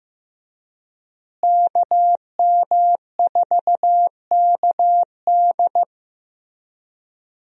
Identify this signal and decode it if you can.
Morse code, heard as a keyed tone, about 15 wpm, 700 Hz. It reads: KM4KD
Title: USA Call Signs - 5 Characters - Repeated 3x - 15wpm